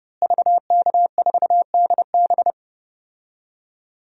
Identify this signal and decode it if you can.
Morse code, heard as a keyed tone, about 30 wpm, 700 Hz. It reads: VK4DB
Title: DXCC-100 Call Signs - 5 Characters - Repeated 3x - 30wpm